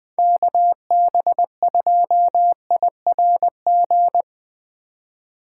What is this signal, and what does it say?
Morse code, heard as a keyed tone, about 20 wpm, 700 Hz. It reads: KB2IRG
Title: Speed Racing - USA 2x3 Call Signs 17wpm